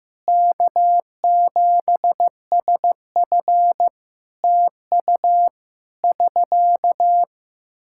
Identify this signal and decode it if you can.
Morse code, heard as a keyed tone, about 16 wpm, 700 Hz. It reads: K7SF TU <SK>
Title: Practice QSO - Exchange 4 - Ending - 15wpm